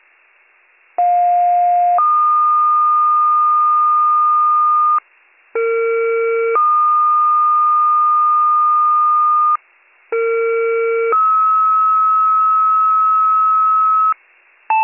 Two-tone page, heard as a one-second tone, then a three-second tone, about 710 then 1220 Hz, about 460 then 1220 Hz, about 460 then 1290 Hz. It is heard from 1.0 to 14.1 s.